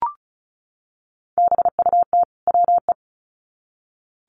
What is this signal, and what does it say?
Morse code, heard as a keyed tone, about 34 wpm, 700 Hz. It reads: BUT WE
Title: Sets of 2 Words 35wpm